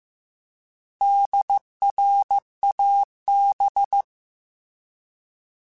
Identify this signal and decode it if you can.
Morse code, heard as a keyed tone, about 15 wpm, 800 Hz. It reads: DRAB
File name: Alphabetical / Alphabetical mp3s / Ds @13